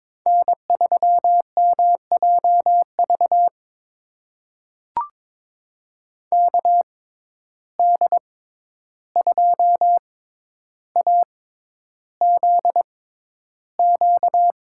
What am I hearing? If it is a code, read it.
Morse code, heard as a keyed tone, about 22 wpm, spaced as for 11 wpm, 700 Hz. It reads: N3MJV KD2AZQ
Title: USA Call Signs - 6x Character Spacing - 22wpm